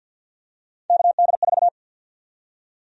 Morse code, heard as a keyed tone, about 53 wpm, 700 Hz. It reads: KD4
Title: USA Call Sign Prefixes - Repeated 2x - 50wpm